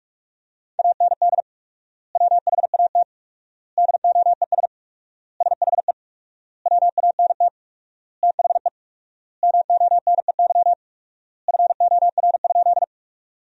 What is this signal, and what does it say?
Morse code, heard as a keyed tone, about 46 wpm, 700 Hz. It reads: AND WHAT DOES SHE WANT THE MONEY FOR?
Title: 4x Word-Spacing - Sentences from Top 400 Words - 45wpm